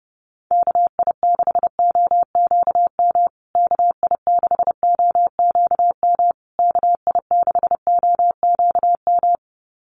Morse code, heard as a keyed tone, about 30 wpm, 700 Hz. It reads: KI6OQM KI6OQM KI6OQM
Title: USA Call Signs Repeated 3x 30wpm